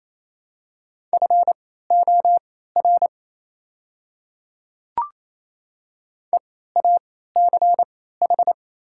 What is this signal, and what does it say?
Morse code, heard as a keyed tone, about 28 wpm, spaced as for 16 wpm, 700 Hz. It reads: FOR EACH
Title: 3x Character Spacing - Top 100 Words - 28wpm